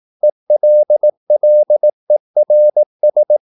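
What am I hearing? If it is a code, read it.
Morse code, heard as a keyed tone, about 18 wpm, 600 Hz. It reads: ELLERS
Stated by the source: Slow (QRS) Morse practice at 18 wpm, CW med Gnister.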